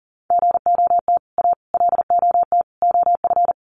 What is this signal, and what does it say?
Morse code, heard as a keyed tone, about 40 wpm, 700 Hz. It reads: GOT A LOT OF
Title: Sets of 4 Words 40wpm